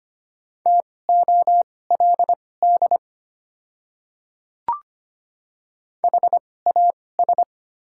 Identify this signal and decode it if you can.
Morse code, heard as a keyed tone, about 25 wpm, spaced as for 18 wpm, 700 Hz. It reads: TOLD HAS
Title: 2x Character Spacing - Top 500 Words - 25wpm